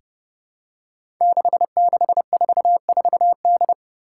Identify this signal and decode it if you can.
Morse code, heard as a keyed tone, about 30 wpm, 700 Hz. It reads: 6644D
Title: Five Challenging Letter-Number Combinations 30wpm